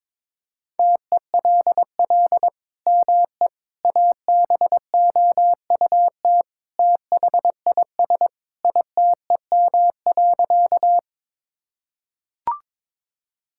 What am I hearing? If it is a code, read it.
Morse code, heard as a keyed tone, about 21 wpm, 700 Hz. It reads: TELL ME ABOUT THIS ITEM.
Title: Sentences from Top 1000 Words - Review Part 6 - 22wpm